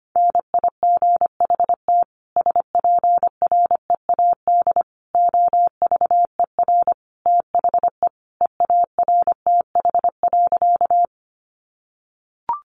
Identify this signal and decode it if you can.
Morse code, heard as a keyed tone, about 25 wpm, 700 Hz. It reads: NIGHT SPREAD OVER THE EARTH.
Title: Sentences from Top 1200 Words 25wpm